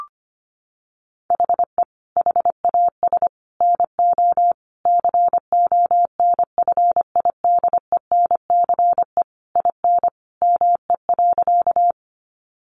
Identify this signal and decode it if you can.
Morse code, heard as a keyed tone, about 25 wpm, 700 Hz. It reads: HE HAS NO CONFIDENCE IN ME.
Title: Sentences from Top 1500 Words 25wpm